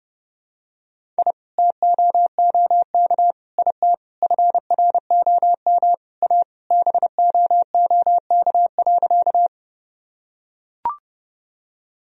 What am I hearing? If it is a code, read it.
Morse code, heard as a keyed tone, about 31 wpm, 700 Hz. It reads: I TOOK IT FROM A BOOK.
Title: Sentences from Top 1000 Words - Review Part 1 - 30wpm